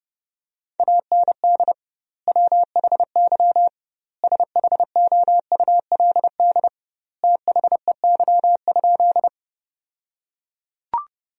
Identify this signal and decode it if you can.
Morse code, heard as a keyed tone, about 32 wpm, 700 Hz. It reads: AND WHY SHOULD THEY?
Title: 2x Word-Spacing - Sentences from Top 300 Words - 30wpm